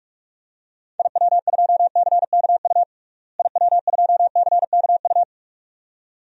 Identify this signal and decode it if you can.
Morse code, heard as a keyed tone, about 46 wpm, 700 Hz. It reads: IW2CKU IW2CKU
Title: DXCC-100 Call Signs - 6 Characters - Repeated 2x - 45wpm